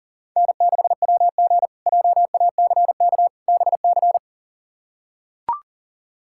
Morse code, heard as a keyed tone, about 37 wpm, 700 Hz. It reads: N6WG JACK BC
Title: K1USN SST QSO Practice - 40wpm